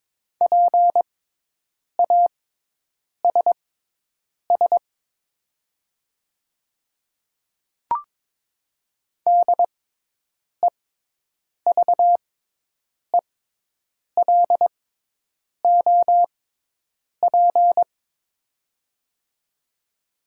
Morse code, heard as a keyed tone, about 22 wpm, spaced as for 8 wpm, 700 Hz. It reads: PASS DEVELOP
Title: 6x Character Spacing - Top 500 Words - 22wpm